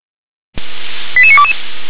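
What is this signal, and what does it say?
Five-tone selective call (ZVEI), heard as quick runs of five tones, tones of about 70 ms. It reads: 88922
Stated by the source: ZVEI five-tone call received by SDR.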